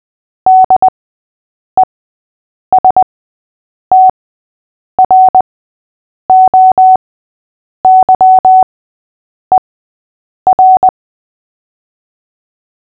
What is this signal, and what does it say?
Morse code, heard as a keyed tone, about 20 wpm, spaced as for 8 wpm, 750 Hz. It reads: DESTROYER
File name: cw / fists / Short-Story-3-8wpm